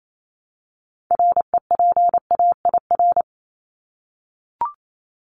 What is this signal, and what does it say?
Morse code, heard as a keyed tone, about 28 wpm, 700 Hz. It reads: REPAIR
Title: Top 1501-1750 Words 28wpm